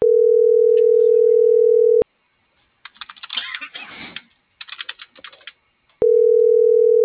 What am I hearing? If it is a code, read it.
Ringback tone (North America), 440 + 480 Hz.